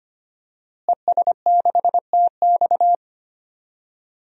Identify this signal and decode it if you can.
Morse code, heard as a keyed tone, about 25 wpm, 700 Hz. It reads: ES6TX
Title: DXCC-100 Call Signs - 5 Characters - Repeated 2x - 25wpm